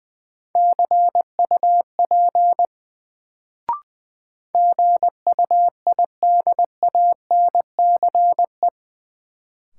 Morse code, heard as a keyed tone, about 20 wpm, 700 Hz. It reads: CUP GUIDANCE
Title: Sentences from Top 1600 Words - New Word Drill Alt - 20wpm